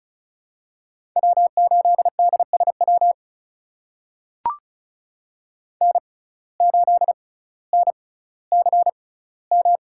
Morse code, heard as a keyed tone, about 36 wpm, spaced as for 18 wpm, 700 Hz. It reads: W8DSW N8NCM
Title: USA Call Signs - 5 Characters - 6x Character Spacing - 35wpm